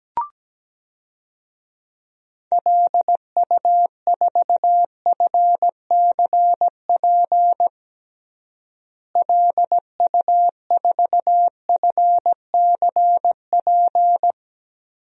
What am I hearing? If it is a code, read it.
Morse code, heard as a keyed tone, about 17 wpm, 700 Hz. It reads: LU4FCP LU4FCP
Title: DXCC-100 Call Signs - 6 Characters - Repeated 2x - 17wpm